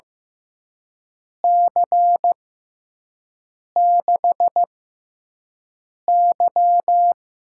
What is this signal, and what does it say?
Morse code, heard as a keyed tone, about 15 wpm, spaced as for 5 wpm, 700 Hz. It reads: C6Y
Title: USA Call Signs - 6 Characters - 6x Character Spacing - 15wpm